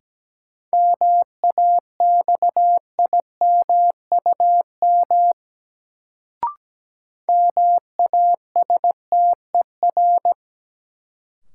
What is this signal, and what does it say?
Morse code, heard as a keyed tone, about 17 wpm, 700 Hz. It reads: MAXIMUM MASTER
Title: Sentences from Top 1300 Words - New Word Drill - 17wpm